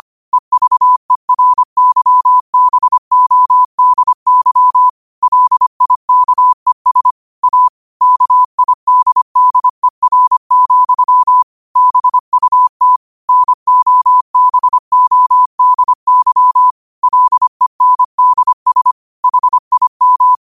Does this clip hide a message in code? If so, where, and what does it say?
Yes: Morse code from 0.3 to 20.4 s, heard as a keyed tone, about 25 wpm, 1000 Hz. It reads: EVERYBODY LIKES A KIDDER, BUT NOBODY LENDS HIM